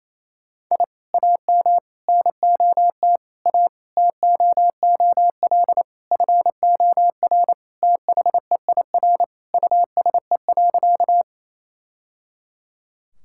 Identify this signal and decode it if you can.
Morse code, heard as a keyed tone, about 28 wpm, 700 Hz. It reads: I AM NOT A TOOL FOR THEIR USE.
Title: Sentences from Top 1000 Words - Review Part 6 - 28wpm